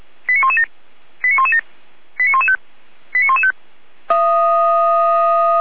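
Five-tone selective call (ZVEI), heard as quick runs of five tones, tones of about 80 ms. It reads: 89117 89117 89115 89115
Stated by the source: ZVEI five-tone call received by SDR.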